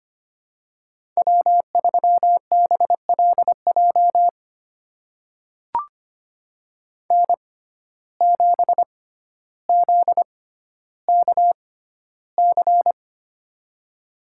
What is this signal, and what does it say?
Morse code, heard as a keyed tone, about 25 wpm, spaced as for 13 wpm, 700 Hz. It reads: W3BLJ N7ZKC
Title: USA Call Signs - 6x Character Spacing - 25wpm